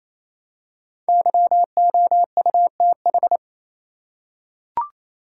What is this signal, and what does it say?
Morse code, heard as a keyed tone, about 28 wpm, 700 Hz. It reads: YOUTH
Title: Five Letter Words - 28wpm